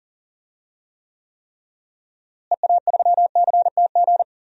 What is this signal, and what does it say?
Morse code, heard as a keyed tone, about 40 wpm, 700 Hz. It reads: EA3CTG
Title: DXCC-100 Call Signs - 6 Characters - Repeated 3x - 40wpm